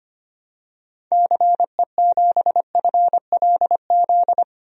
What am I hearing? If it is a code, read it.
Morse code, heard as a keyed tone, about 25 wpm, 700 Hz. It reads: CE7FLZ
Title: DXCC-100 Call Signs - 6 Characters - 25wpm